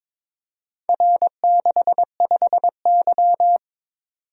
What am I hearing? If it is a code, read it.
Morse code, heard as a keyed tone, about 23 wpm, 700 Hz. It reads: R65Y
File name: practice / 22wpm / Four Letter-Number Combinations - 22wpm